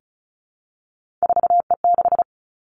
Morse code, heard as a keyed tone, about 36 wpm, 700 Hz. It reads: 4E6
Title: Three Challenging Letter-Number Combinations 35wpm